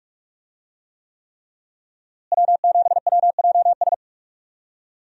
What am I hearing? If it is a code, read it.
Morse code, heard as a keyed tone, about 45 wpm, 700 Hz. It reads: W7WJS
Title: USA Call Signs - 5 Characters - Repeated 3x - 45wpm